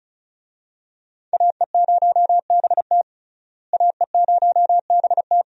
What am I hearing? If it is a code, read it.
Morse code, heard as a keyed tone, about 36 wpm, 700 Hz. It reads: AE0BT AE0BT
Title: USA Call Signs - 5 Characters - Repeated 2x - 35wpm